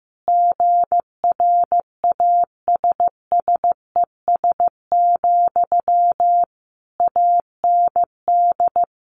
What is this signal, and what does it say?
Morse code, heard as a keyed tone, about 15 wpm, 700 Hz. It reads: GRASSES, AND
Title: A Sound Of Thunder 15wpm